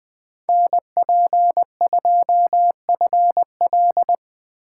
Morse code, heard as a keyed tone, about 23 wpm, 700 Hz. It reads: NP2FL
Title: Speed Racing - USA Call Signs 20wpm